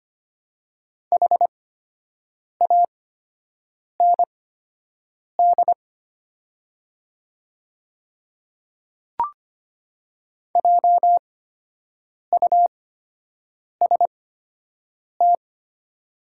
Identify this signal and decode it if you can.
Morse code, heard as a keyed tone, about 25 wpm, spaced as for 7 wpm, 700 Hz. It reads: HAND JUST
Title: 8x Character Spacing - Top 200 Words - 25wpm